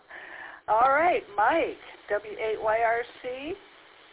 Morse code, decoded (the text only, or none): none